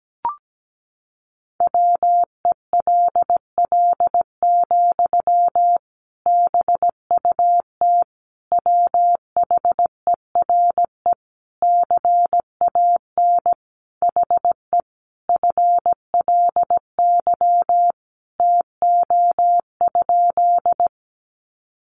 Morse code, heard as a keyed tone, about 17 wpm, 700 Hz. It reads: WELL, BUT WHERE CAN HE FLY TO?
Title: Sentences from Top 500 Words 17wpm